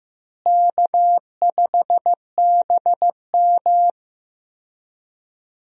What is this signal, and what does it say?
Morse code, heard as a keyed tone, about 15 wpm, 700 Hz. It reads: K5BM
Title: USA Call Signs - 4 Characters - Repeated 3x - 15wpm